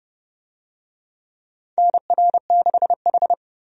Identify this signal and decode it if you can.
Morse code, heard as a keyed tone, about 30 wpm, 700 Hz. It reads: NR6H
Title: USA Call Signs - 4 Characters - 30wpm